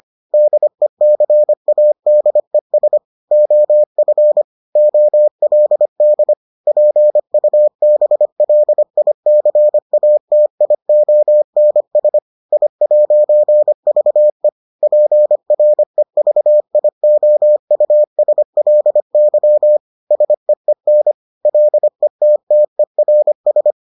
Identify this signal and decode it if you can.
Morse code, heard as a keyed tone, about 25 wpm, 600 Hz. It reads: DECADES OF OLD PUBLICATIONS I'VE PREVIOUSLY SEEN LETTERS